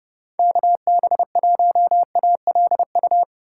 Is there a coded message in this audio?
Yes: Morse code from 0.4 to 3.2 s, heard as a keyed tone, about 30 wpm, 700 Hz. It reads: KB1ALU